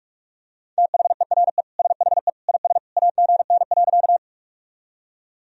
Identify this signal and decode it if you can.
Morse code, heard as a keyed tone, about 47 wpm, 700 Hz. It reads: THERE SHE IS AGN.
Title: Sentences from Top 200 Words and Abbreviations 45wpm